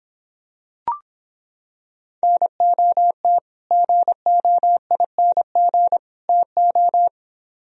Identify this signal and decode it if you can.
Morse code, heard as a keyed tone, about 26 wpm, 700 Hz. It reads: NOT GOING TO